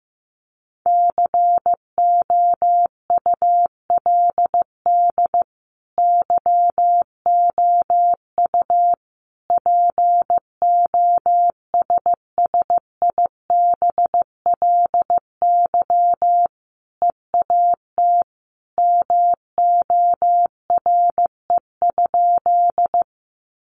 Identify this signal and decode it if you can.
Morse code, heard as a keyed tone, about 15 wpm, 700 Hz. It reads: COULD YOU POSSIBLY EAT MORE?
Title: Sentences from Top 1300 Words 15wpm